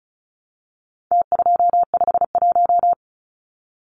Morse code, heard as a keyed tone, about 34 wpm, 700 Hz. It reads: T251
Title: Numbers - Contest Style - 4-Digit 35wpm